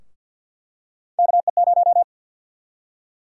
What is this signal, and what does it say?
Morse code, heard as a keyed tone, about 47 wpm, 700 Hz. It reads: KE0
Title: USA Call Sign Prefixes - Repeated 2x - 50wpm